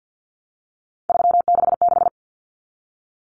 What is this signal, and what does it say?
Morse code, heard as a keyed tone, about 50 wpm, 700 Hz. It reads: 366
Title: Numbers - 3-Digit 50wpm